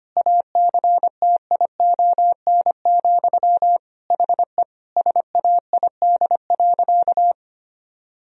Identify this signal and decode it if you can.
Morse code, heard as a keyed tone, about 25 wpm, 700 Hz. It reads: ACTION, HE SAID.
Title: Sentences from Top 1000 Words - Review Part 4 - 25wpm